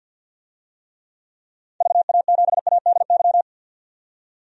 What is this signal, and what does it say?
Morse code, heard as a keyed tone, about 50 wpm, 700 Hz. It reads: VA7ADY
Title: DXCC-100 Call Signs - 6 Characters - 50wpm